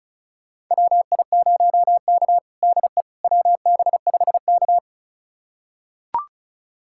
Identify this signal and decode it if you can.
Morse code, heard as a keyed tone, about 34 wpm, 700 Hz. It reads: WI0K DE WB5K
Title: USA Callsign DE Callsign - 4 Characters - 35wpm